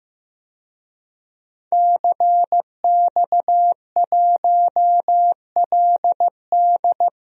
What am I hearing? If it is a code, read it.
Morse code, heard as a keyed tone, about 16 wpm, 700 Hz. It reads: CX1LD
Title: DXCC-100 Call Signs - 5 Characters - Repeated 3x - 15wpm